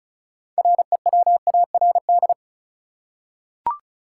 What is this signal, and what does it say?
Morse code, heard as a keyed tone, about 36 wpm, 700 Hz. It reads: REWARD
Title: Six Letter Words - 35wpm